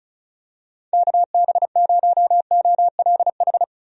Morse code, heard as a keyed tone, about 36 wpm, 700 Hz. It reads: KB0OLH
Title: USA Call Signs - 6 Characters - 35wpm